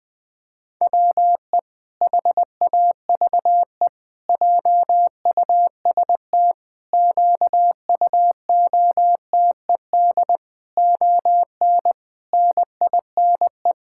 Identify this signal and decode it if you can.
Morse code, heard as a keyed tone, about 21 wpm, 700 Hz. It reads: WE HAVE JUST QUOTED ON NINE
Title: Pangram Challenge v2 - 20wpm